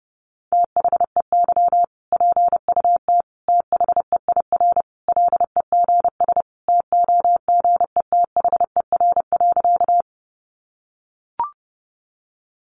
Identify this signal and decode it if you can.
Morse code, heard as a keyed tone, about 30 wpm, 700 Hz. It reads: THEY PUT THEIR LEGS TOGETHER.
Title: Easy Phrases 30wpm